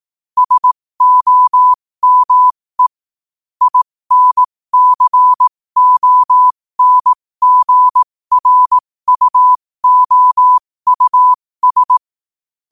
Morse code, heard as a keyed tone, about 18 wpm, spaced as for 15 wpm, 1000 Hz. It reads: SOME INCONGRUOUS
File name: qotdfiles / 15wpm